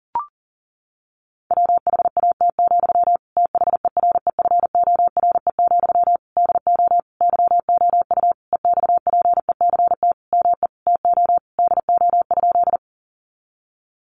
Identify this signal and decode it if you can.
Morse code, heard as a keyed tone, about 40 wpm, 700 Hz. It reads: WHAT, THEREFORE, DO YOU EXPECT ME TO DO?
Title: Sentences from Top 1100 Words 40wpm